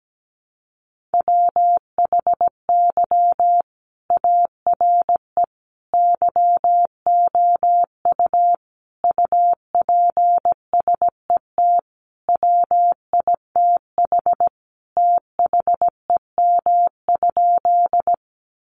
Morse code, heard as a keyed tone, about 17 wpm, 700 Hz. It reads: WHY ARE YOU UPSET WITH THEM?
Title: Sentences from Top 800 Words 17wpm